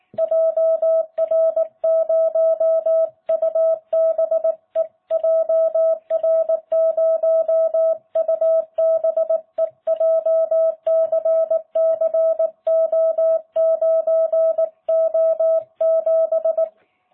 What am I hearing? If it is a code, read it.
Morse code, heard as a keyed tone, about 19 wpm, 650 Hz. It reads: JR0UBE JR0UBE JCCO9O7